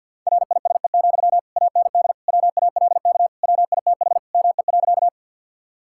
Morse code, heard as a keyed tone, about 50 wpm, 700 Hz. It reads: RISE, AND WALK WITH ME.
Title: Sentences from Top 1000 Words - Review Part 6 - 50wpm